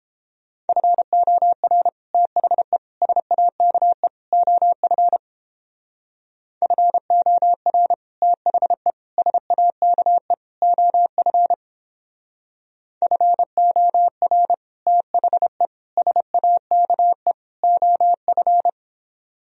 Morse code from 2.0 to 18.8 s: THE SAKE OF FOR THE SAKE OF FOR THE SAKE OF

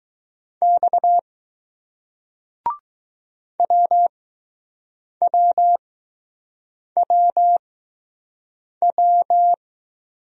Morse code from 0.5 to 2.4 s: X

